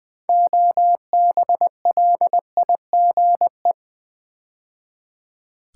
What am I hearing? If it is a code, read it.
Morse code, heard as a keyed tone, about 20 wpm, 700 Hz. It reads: OBLIGE
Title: Build A Word - 3 Letters At A Time - 6-Letter Words - 20wpm